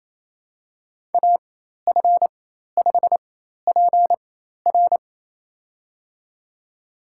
Morse code, heard as a keyed tone, about 28 wpm, spaced as for 13 wpm, 700 Hz. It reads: AF5PR
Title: USA Call Signs - 5 Characters - 4x Character Spacing - 28wpm